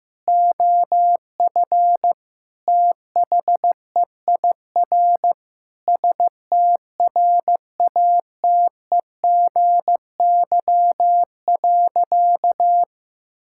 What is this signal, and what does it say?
Morse code, heard as a keyed tone, about 15 wpm, 700 Hz. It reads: OF THEIR STRATEGY.
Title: Sentences from Top 1000 Words - Review Part 5 - 15wpm